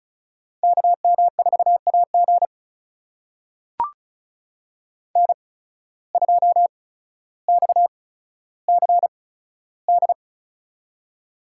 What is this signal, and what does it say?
Morse code, heard as a keyed tone, about 34 wpm, spaced as for 15 wpm, 700 Hz. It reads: KM4AG N2XCD
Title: USA Call Signs - 5 Characters - 8x Character Spacing - 35wpm